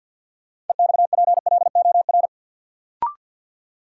Morse code, heard as a keyed tone, about 47 wpm, 700 Hz. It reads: EXPLOR
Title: Sentences from Top 1100 Words - New Word Drill - 50wpm